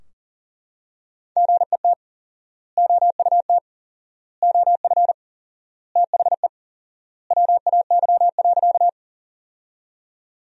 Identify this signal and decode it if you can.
Morse code, heard as a keyed tone, about 40 wpm, 700 Hz. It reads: GET OUT OF THE WAY.
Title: 4x Word-Spacing - Sentences from Top 200 Words - 40wpm